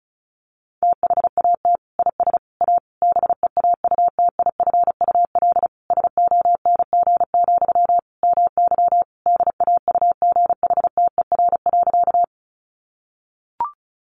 Morse code, heard as a keyed tone, about 34 wpm, 700 Hz. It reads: THAT IS A BEAUTIFUL SONG, MY DAUGHTER.
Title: Sentences from Top 1500 Words 35wpm